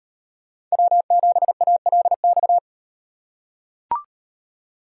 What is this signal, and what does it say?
Morse code, heard as a keyed tone, about 38 wpm, 700 Hz. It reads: W7ALX